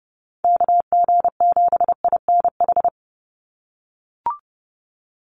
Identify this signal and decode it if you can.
Morse code, heard as a keyed tone, about 30 wpm, 700 Hz. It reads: KG7INH
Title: USA Call Signs 30wpm